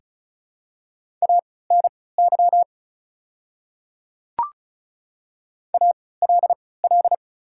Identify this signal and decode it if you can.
Morse code, heard as a keyed tone, about 36 wpm, spaced as for 20 wpm, 700 Hz. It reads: ANY ALL
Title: 3x Character Spacing - Top 200 Words - 35wpm